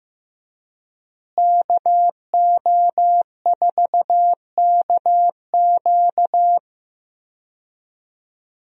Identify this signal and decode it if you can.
Morse code, heard as a keyed tone, about 15 wpm, 700 Hz. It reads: KO4KQ
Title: USA Call Signs - 5 Characters - Repeated 3x - 15wpm